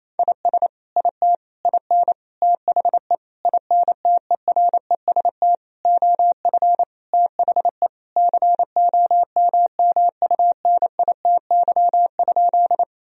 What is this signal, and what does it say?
Morse code, heard as a keyed tone, about 28 wpm, 700 Hz. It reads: IS IT IN THE INTEREST OF THE COMMUNITY?
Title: Sentences from Top 1000 Words - Review Part 4 - 28wpm